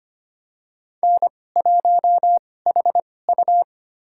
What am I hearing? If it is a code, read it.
Morse code, heard as a keyed tone, about 25 wpm, spaced as for 18 wpm, 700 Hz. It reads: N1HU
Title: USA Call Signs - 4 Characters - 2x Character Spacing - 25wpm